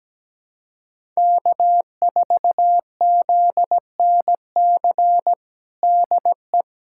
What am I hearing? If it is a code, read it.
Morse code, heard as a keyed tone, about 17 wpm, 700 Hz. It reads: K4ZNC DE
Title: USA Callsign DE Callsign 17wpm